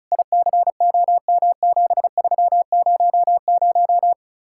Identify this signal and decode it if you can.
Morse code, heard as a keyed tone, about 37 wpm, 700 Hz. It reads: ICOM7300
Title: Speed-Racing Common QSO Phrases - 30wpm